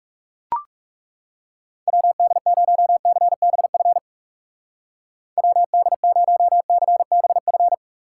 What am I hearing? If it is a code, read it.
Morse code, heard as a keyed tone, about 36 wpm, 700 Hz. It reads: WD0CBF WD0CBF
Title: Speed Racing - USA Call Signs 30wpm